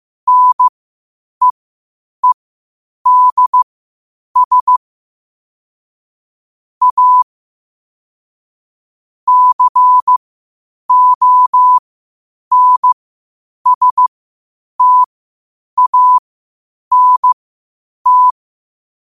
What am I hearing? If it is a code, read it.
Morse code, heard as a keyed tone, about 15 wpm, spaced as for 8 wpm, 1000 Hz. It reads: NEEDS A CONSTANT
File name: qotdfiles / 8wpm